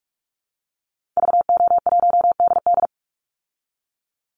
Morse code, heard as a keyed tone, about 44 wpm, 700 Hz. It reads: VO1DD
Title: Call Signs Repeated 3x - 45wpm